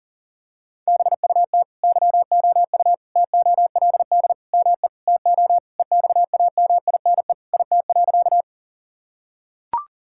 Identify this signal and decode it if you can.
Morse code, heard as a keyed tone, about 40 wpm, 700 Hz. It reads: BUT YOU TOLD ME TO EXAMINE IT.
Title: Sentences from Top 1600 Words 40wpm